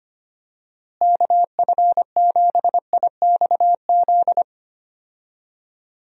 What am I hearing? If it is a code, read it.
Morse code, heard as a keyed tone, about 25 wpm, 700 Hz. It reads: KF7IXZ
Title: USA Call Signs - 6 Characters - Repeated 2x - 25wpm